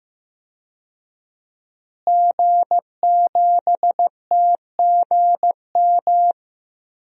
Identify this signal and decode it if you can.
Morse code, heard as a keyed tone, about 15 wpm, 700 Hz. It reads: G7TGM
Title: DXCC-100 Call Signs - 5 Characters - Repeated 2x - 15wpm